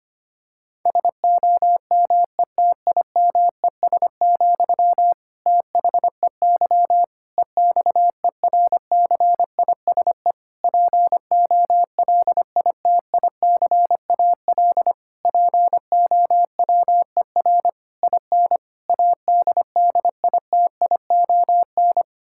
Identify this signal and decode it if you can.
Morse code, heard as a keyed tone, about 25 wpm, 700 Hz. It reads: SOMETIMES, THEY EXERCISE POLITICAL POWER IN ADDITION
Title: Sentences from Top 1000 Words - Review Part 6 - 25wpm